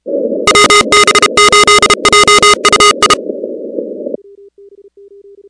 Morse code, heard as a keyed tone, about 32 wpm, 1240 Hz, with noise. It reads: WB8JUI
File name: contests / 2019-cw-wpx-kt5j-k5pi- -2330 0-16